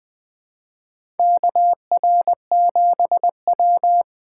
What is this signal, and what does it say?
Morse code, heard as a keyed tone, about 21 wpm, 700 Hz. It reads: KR7W
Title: K1USN SST QSO Practice - 20wpm